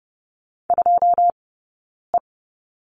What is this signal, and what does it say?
Morse code, heard as a keyed tone, about 30 wpm, spaced as for 9 wpm, 690 Hz. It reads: 2E
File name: Lesson 17d - Letter u - Two Characters 30wpm